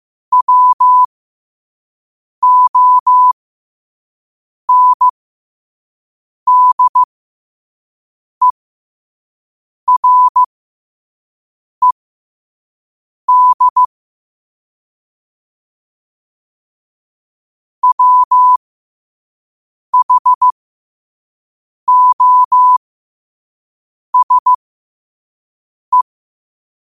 Morse code, heard as a keyed tone, about 15 wpm, spaced as for 5 wpm, 1000 Hz. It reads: WONDERED WHOSE
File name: qotdfiles / 5wpm